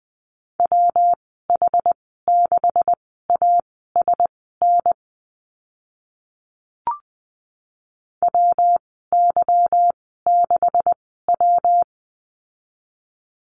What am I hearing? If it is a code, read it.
Morse code, heard as a keyed tone, about 20 wpm, spaced as for 14 wpm, 700 Hz. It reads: WH6ASN WY6W
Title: 167 Lesson 27g - Number 7 - Call Signs 20wpm